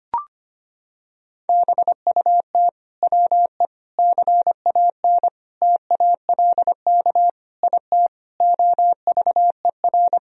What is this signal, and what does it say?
Morse code, heard as a keyed tone, about 25 wpm, 700 Hz. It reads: BUT WE CAN TALK IT OVER
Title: Sentences from Top 1000 Words - Review Part 4 - 25wpm